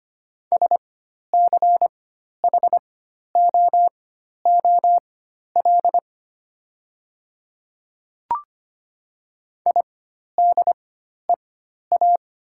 Morse code, heard as a keyed tone, about 25 wpm, spaced as for 12 wpm, 700 Hz. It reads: SCHOOL IDEA
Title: 4x Character Spacing - Top 300 Words - 25wpm